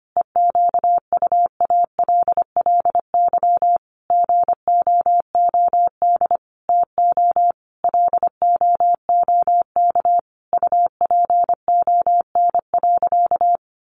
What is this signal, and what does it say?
Morse code, heard as a keyed tone, about 25 wpm, 700 Hz. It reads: EQUALLY GOOD TO LOOK UPON.
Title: Sentences from Top 1400 Words 25wpm